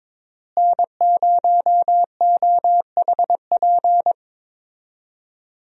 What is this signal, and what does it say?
Morse code, heard as a keyed tone, about 22 wpm, 700 Hz. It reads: N0OHP
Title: USA Call Signs - 5 Characters - Repeated 3x - 22wpm